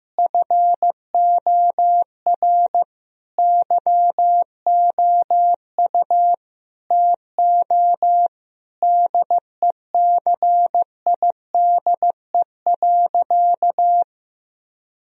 Morse code, heard as a keyed tone, about 15 wpm, 700 Hz. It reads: FOR YOU TO DECIDE.
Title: Sentences from Top 1000 Words - Review Part 3 - 15wpm